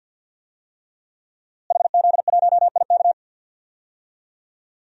Morse code, heard as a keyed tone, about 53 wpm, 700 Hz. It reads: HZ1IK
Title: DXCC-100 Call Signs - 5 Characters - Repeated 3x - 50wpm